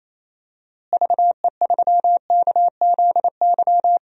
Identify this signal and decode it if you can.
Morse code, heard as a keyed tone, about 28 wpm, 700 Hz. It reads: VE3KZY
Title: DXCC-100 Call Signs - 6 Characters - Repeated 2x - 28wpm